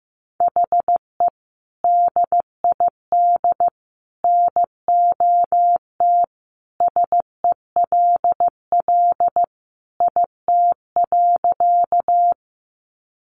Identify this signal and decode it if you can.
Morse code, heard as a keyed tone, about 15 wpm, 700 Hz. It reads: HE DID NOT SELL IT.
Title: Sentences from Top 1100 Words 15wpm